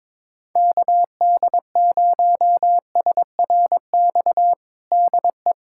Morse code, heard as a keyed tone, about 22 wpm, 700 Hz. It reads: KD0SRX DE
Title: USA Callsign DE Callsign - 6 Characters - 22wpm